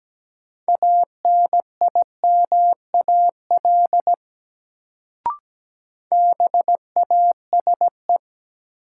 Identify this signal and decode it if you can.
Morse code, heard as a keyed tone, about 17 wpm, 700 Hz. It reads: ANIMAL BASE
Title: Sentences from Top 1000 Words - Review Part 1 - New Word Drill - 17wpm